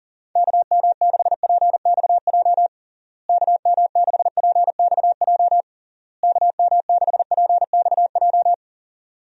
Morse code, heard as a keyed tone, about 37 wpm, 700 Hz. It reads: KM6PXJ KM6PXJ KM6PXJ
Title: USA Call Signs - 6 Characters - Repeated 3x - 40wpm